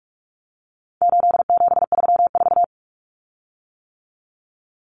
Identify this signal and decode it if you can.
Morse code, heard as a keyed tone, about 44 wpm, 700 Hz. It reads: 8734
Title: Common Cut Numbers - 4-Digit 45wpm